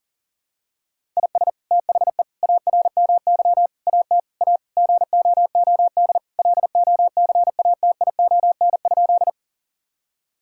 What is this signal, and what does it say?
Morse code, heard as a keyed tone, about 37 wpm, 700 Hz. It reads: IS THE ARMY AT A GOOD LOCATION?
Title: Sentences from Top 1000 Words - Review Part 6 - 40wpm